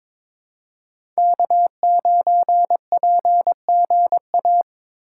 Morse code, heard as a keyed tone, about 21 wpm, 700 Hz. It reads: K9PGA
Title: USA Call Signs - 3x Character Spacing - 22wpm